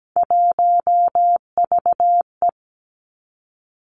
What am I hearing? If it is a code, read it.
Morse code, heard as a keyed tone, about 17 wpm, 700 Hz. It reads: 1VE
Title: Three Challenging Letter-Number Combinations 17wpm